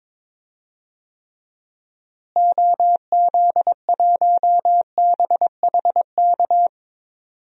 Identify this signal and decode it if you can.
Morse code, heard as a keyed tone, about 22 wpm, 700 Hz. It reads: OZ1BHK
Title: DXCC-100 Call Signs - 6 Characters - Repeated 3x - 22wpm